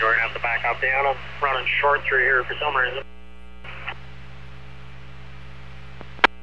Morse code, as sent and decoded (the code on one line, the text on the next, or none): none